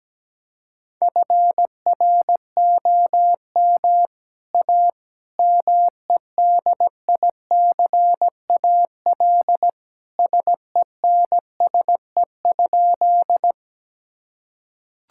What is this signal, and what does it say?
Morse code, heard as a keyed tone, about 17 wpm, 700 Hz. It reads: FROM A MEDICAL SENSE?
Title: Sentences from Top 1000 Words - Review Part 5 - 17wpm